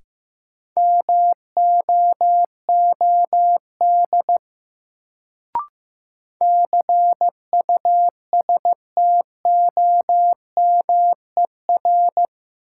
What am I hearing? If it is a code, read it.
Morse code, heard as a keyed tone, about 15 wpm, 700 Hz. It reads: MOOD CUSTOMER
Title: Sentences from Top 1100 Words - New Word Drill - 15wpm